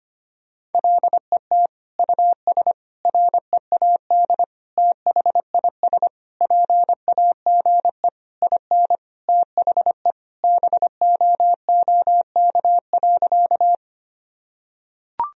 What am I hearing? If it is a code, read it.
Morse code, heard as a keyed tone, about 25 wpm, 700 Hz. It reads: LET US READ THIS PAGE IN THE BOOK.
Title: Sentences from Top 1000 Words - Review Part 1 - 25wpm